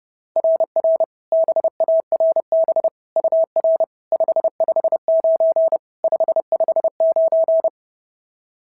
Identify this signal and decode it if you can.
Morse code, heard as a keyed tone, about 30 wpm, 690 Hz. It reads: RR BARB UR 559 559
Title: SOTA QSO Practice 30wpm